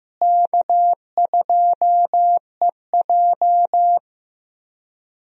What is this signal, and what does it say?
Morse code, heard as a keyed tone, about 15 wpm, 700 Hz. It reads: K2EJ
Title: USA Call Signs - 4 Characters - Repeated 3x - 15wpm